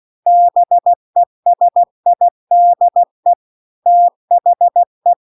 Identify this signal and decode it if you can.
Morse code, heard as a keyed tone, about 16 wpm, 700 Hz. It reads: BESIDE THE
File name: CW-The-Big-Twoheaded-River-2-16WPM